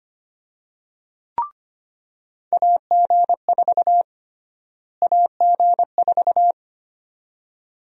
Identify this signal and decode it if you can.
Morse code, heard as a keyed tone, about 25 wpm, 700 Hz. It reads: AG4 AG4
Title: USA Call Sign Prefixes - Repeated 2x - 25wpm